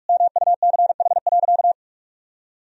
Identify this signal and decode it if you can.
Morse code, heard as a keyed tone, about 46 wpm, 700 Hz. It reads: MUCH.
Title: Sentences from Top 1000 Words - Review Part 3 - 45wpm